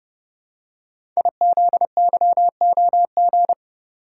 Morse code, heard as a keyed tone, about 29 wpm, 700 Hz. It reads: IZYOG
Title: Code Groups - 5 Letters - 30wpm